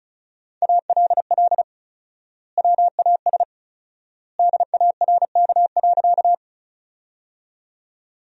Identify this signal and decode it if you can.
Morse code, heard as a keyed tone, about 34 wpm, 700 Hz. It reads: ALL WAS DARK.
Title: 4x Word-Spacing - Sentences from Top 500 Words - 35wpm